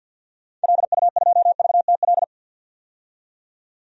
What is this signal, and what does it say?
Morse code, heard as a keyed tone, about 46 wpm, 700 Hz. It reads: LU2VTL
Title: DXCC-100 Call Signs - 6 Characters - 50wpm